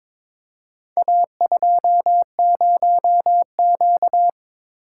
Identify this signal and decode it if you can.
Morse code, heard as a keyed tone, about 22 wpm, 700 Hz. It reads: A20Q
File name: practice / 22wpm / Four Letter-Number Combinations - 22wpm